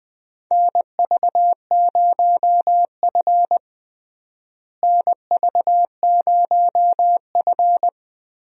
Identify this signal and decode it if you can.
Morse code, heard as a keyed tone, about 20 wpm, 700 Hz. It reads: NV0F NV0F
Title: USA Call Signs - 4 Characters - Repeated 3x - 20wpm